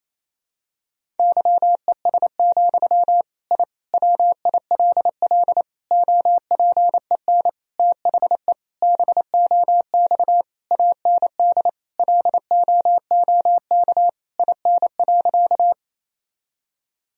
Morse code, heard as a keyed tone, about 28 wpm, 700 Hz. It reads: YES, I WILL OPEN THE BOX AND LOOK IN.
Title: Sentences from Top 1000 Words - Review Part 2 - 28wpm